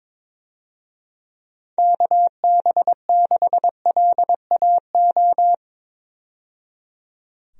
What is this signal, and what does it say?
Morse code, heard as a keyed tone, about 22 wpm, 700 Hz. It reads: KB6LAO
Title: USA Call Signs - 6 Characters - 22wpm